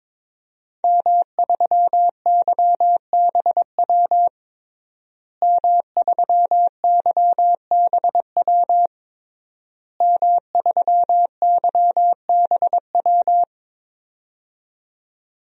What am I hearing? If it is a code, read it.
Morse code, heard as a keyed tone, about 21 wpm, 700 Hz. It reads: M3YBW M3YBW M3YBW
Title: DXCC-100 Call Signs - 5 Characters - Repeated 3x - 22wpm